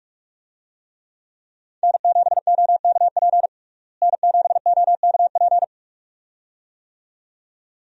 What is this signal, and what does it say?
Morse code, heard as a keyed tone, about 46 wpm, 700 Hz. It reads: N7OKP N7OKP
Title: USA Call Signs - 5 Characters - Repeated 2x - 45wpm